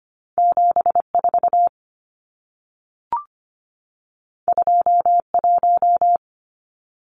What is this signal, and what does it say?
Morse code, heard as a keyed tone, about 25 wpm, 700 Hz. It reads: 74 21
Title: Numbers - 2-Digit 25wpm